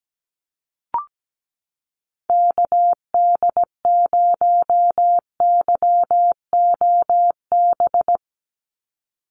Morse code, heard as a keyed tone, about 17 wpm, 700 Hz. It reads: KD0YOB
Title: NA SOTA Activator Callsigns 17wpm